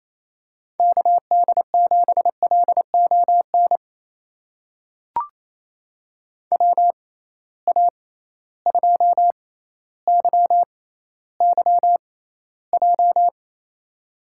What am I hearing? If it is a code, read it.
Morse code, heard as a keyed tone, about 28 wpm, spaced as for 14 wpm, 700 Hz. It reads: KD7LON WA2YYJ
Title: USA Call Signs - 6 Characters - 6x Character Spacing - 28wpm